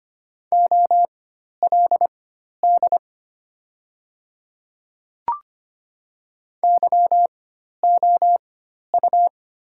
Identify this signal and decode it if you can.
Morse code, heard as a keyed tone, about 25 wpm, spaced as for 12 wpm, 700 Hz. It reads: OLD YOU
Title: 4x Character Spacing - Top 200 Words - 25wpm